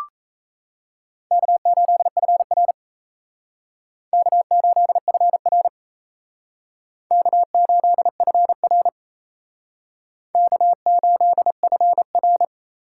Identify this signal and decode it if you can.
Morse code, heard as a keyed tone, about 36 wpm, 700 Hz. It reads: K8FR K8FR K8FR K8FR